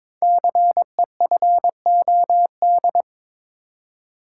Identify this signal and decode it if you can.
Morse code, heard as a keyed tone, about 22 wpm, 700 Hz. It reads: CEFOD
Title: Code Groups - 5 Letters - 22wpm